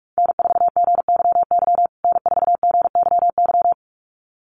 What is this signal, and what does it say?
Morse code, heard as a keyed tone, about 44 wpm, 700 Hz. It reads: N4GYY N4GYY
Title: USA Call Signs Repeated 2x 45wpm